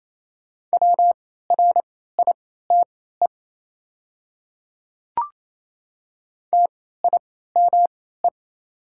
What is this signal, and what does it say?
Morse code, heard as a keyed tone, about 28 wpm, spaced as for 16 wpm, 700 Hz. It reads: WRITE TIME
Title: 3x Character Spacing - Top 200 Words - 28wpm